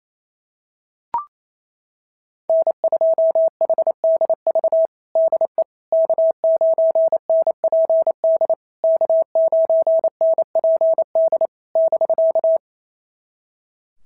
Morse code, heard as a keyed tone, about 28 wpm, 650 Hz. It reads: N2HDV DE K9NPD K9NPD <BK>
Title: SOTA QSO Practice 28wpm